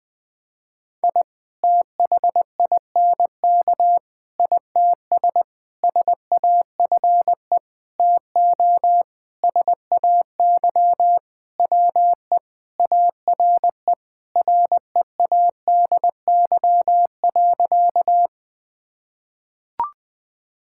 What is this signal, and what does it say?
Morse code, heard as a keyed tone, about 20 wpm, 700 Hz. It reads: I THINK ITS SAFE TO SAY WE ARE READY.
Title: Sentences with 6-Word Phrases - 20wpm